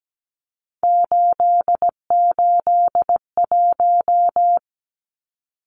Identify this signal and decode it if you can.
Morse code, heard as a keyed tone, about 17 wpm, 700 Hz. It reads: 881
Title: Numbers - 3-Digit 17wpm